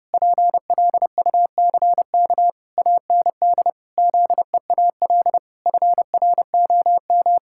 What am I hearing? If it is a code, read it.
Morse code, heard as a keyed tone, about 30 wpm, 700 Hz. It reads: PLUCK AND ZEAL FROM
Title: Pangram Challenge v2 - 30wpm